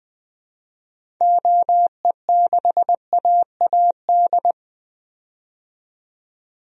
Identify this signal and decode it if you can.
Morse code, heard as a keyed tone, about 20 wpm, 700 Hz. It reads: OE6AAD
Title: DXCC-100 Call Signs - 6 Characters - 20wpm